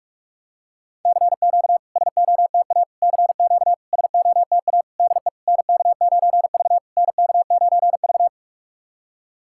Morse code, heard as a keyed tone, about 44 wpm, 700 Hz. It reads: CQ SOTA CQ SOTA DE NK9V NK9V
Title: SOTA QSO Practice 45wpm